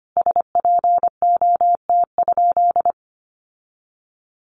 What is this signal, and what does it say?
Morse code, heard as a keyed tone, about 25 wpm, 700 Hz. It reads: SPOT?
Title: Sentences from Top 1100 Words 25wpm